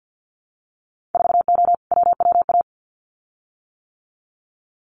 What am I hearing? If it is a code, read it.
Morse code, heard as a keyed tone, about 50 wpm, 700 Hz. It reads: 4O WWA
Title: Winter Field Day Exchanges 50wpm